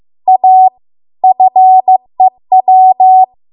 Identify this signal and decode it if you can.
Morse code, heard as a keyed tone, about 15 wpm, 750 Hz. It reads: A FEW